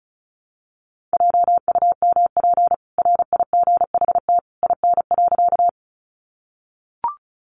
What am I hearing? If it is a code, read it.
Morse code, heard as a keyed tone, about 34 wpm, 700 Hz. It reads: JUMP RIGHT IN.
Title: Sentences from Top 1300 Words 35wpm